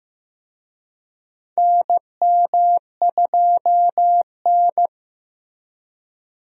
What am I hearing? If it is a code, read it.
Morse code, heard as a keyed tone, about 15 wpm, 700 Hz. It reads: NM2N
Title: USA Call Signs - 4 Characters - Repeated 3x - 15wpm